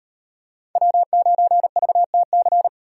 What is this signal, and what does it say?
Morse code, heard as a keyed tone, about 35 wpm, 700 Hz. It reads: W9VTC